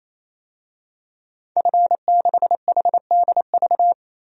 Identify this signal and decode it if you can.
Morse code, heard as a keyed tone, about 27 wpm, 700 Hz. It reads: F6HDV
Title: DXCC-100 Call Signs - 5 Characters - 28wpm